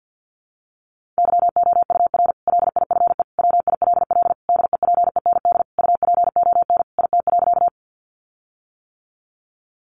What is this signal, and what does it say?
Morse code, heard as a keyed tone, about 50 wpm, 700 Hz. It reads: YOUR LIFE WILL DEPEND UPON IT.
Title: Sentences from Top 1300 Words 50wpm